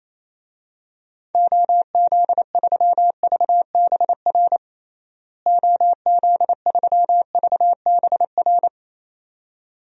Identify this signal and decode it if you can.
Morse code, heard as a keyed tone, about 27 wpm, 700 Hz. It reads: OZ3VBR OZ3VBR
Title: DXCC-100 Call Signs - 6 Characters - Repeated 2x - 28wpm